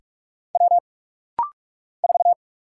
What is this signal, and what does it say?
Morse code, heard as a keyed tone, about 48 wpm, 690 Hz. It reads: W 4
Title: Single Letter-Number - Rapid-Fire 45wpm